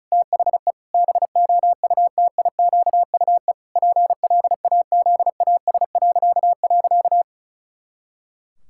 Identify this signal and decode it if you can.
Morse code, heard as a keyed tone, about 34 wpm, 700 Hz. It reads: THE BOUTIQUE PLAZAS..
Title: Pangram Challenge v2 - 35wpm